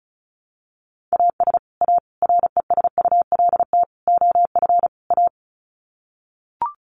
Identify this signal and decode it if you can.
Morse code, heard as a keyed tone, about 34 wpm, 700 Hz. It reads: AS A RESULT OF A
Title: Sets of 5 Words Encore 35wpm